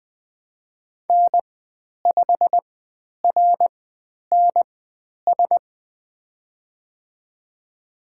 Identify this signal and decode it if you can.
Morse code, heard as a keyed tone, about 20 wpm, spaced as for 10 wpm, 700 Hz. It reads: N5RNS
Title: 095 Lesson 16g - Letter c - Call Signs 20wpm